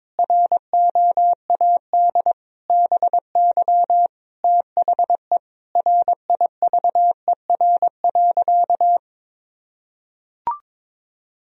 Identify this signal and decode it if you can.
Morse code, heard as a keyed tone, about 22 wpm, 700 Hz. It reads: ROAD BY THE RIVER.
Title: Sentences from Top 1000 Words - Review Part 4 - 22wpm